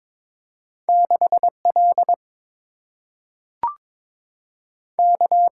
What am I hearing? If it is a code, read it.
Morse code, heard as a keyed tone, about 22 wpm, 700 Hz. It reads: 6 L K
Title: Single-Letter Single-Number 22wpm